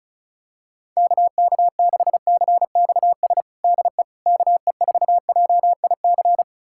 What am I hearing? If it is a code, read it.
Morse code, heard as a keyed tone, about 36 wpm, 700 Hz. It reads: KK6CXS DE KE4JIC
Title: USA Callsign DE Callsign - 6 Characters - 35wpm